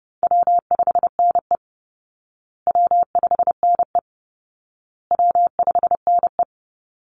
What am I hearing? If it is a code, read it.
Morse code, heard as a keyed tone, about 28 wpm, 700 Hz. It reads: W5NE W5NE W5NE
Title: Call Signs Repeated 2x - 30wpm